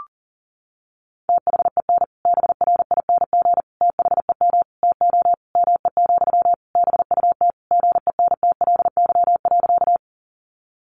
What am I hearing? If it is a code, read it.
Morse code, heard as a keyed tone, about 40 wpm, 700 Hz. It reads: THEN BRING THEM TO ME, BUT GENTLY.
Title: Sentences from Top 1500 Words 40wpm